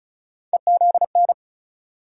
Morse code, heard as a keyed tone, about 34 wpm, 700 Hz. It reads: EZN
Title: USA Call Sign Postfixes - 3 Letters 35wpm